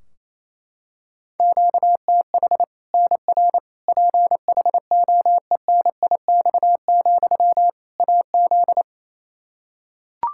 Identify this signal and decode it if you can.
Morse code, heard as a keyed tone, about 28 wpm, 700 Hz. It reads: QTH NR PHOENIX, AZ
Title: Practice QSO - Exchange 1 - Beginning - 28wpm